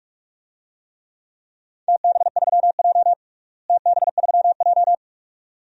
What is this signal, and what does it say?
Morse code, heard as a keyed tone, about 46 wpm, 700 Hz. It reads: TB3J TB3J
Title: DXCC-100 Call Signs - 4 Characters - Repeated 2x - 45wpm